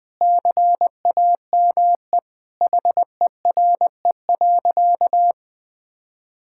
Morse code, heard as a keyed tone, about 20 wpm, 700 Hz. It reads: CAME HERE.
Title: Sentences with 2-Word Phrases - 20wpm